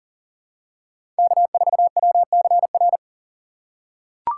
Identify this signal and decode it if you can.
Morse code, heard as a keyed tone, about 40 wpm, 700 Hz. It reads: K4WCR
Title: USA Call Signs - 5 Characters - 3x Character Spacing - 40wpm